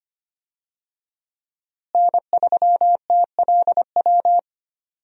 Morse code, heard as a keyed tone, about 25 wpm, 700 Hz. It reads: N3TLW
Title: USA Call Signs - 5 Characters - Repeated 3x - 25wpm